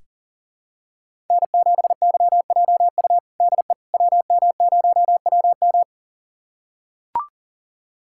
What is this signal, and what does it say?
Morse code, heard as a keyed tone, about 37 wpm, 700 Hz. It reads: N7YJU DE WM0WM
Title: USA Callsign DE Callsign - 5 Characters - 40wpm